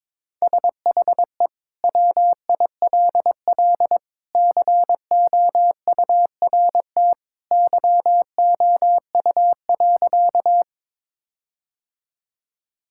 Morse code, heard as a keyed tone, about 22 wpm, 700 Hz. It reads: SHE WILL COURT YOU.
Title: Sentences from Top 1600 Words 22wpm